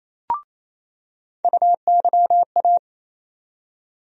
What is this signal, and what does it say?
Morse code, heard as a keyed tone, about 28 wpm, 700 Hz. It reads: UYA
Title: USA Call Sign Postfixes - 3 Letters 28wpm